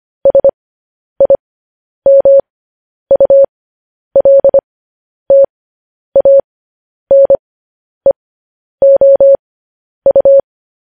Morse code, heard as a keyed tone, about 25 wpm, spaced as for 10 wpm, 550 Hz. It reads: SIMULTANEOU